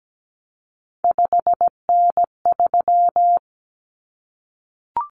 Morse code, heard as a keyed tone, about 17 wpm, 700 Hz. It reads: 5N3
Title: Common Cut Numbers - 3-Digit 17wpm